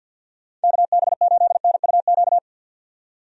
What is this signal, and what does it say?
Morse code, heard as a keyed tone, about 48 wpm, 700 Hz. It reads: KB8NUQ
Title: USA Call Signs - 6 Characters - 50wpm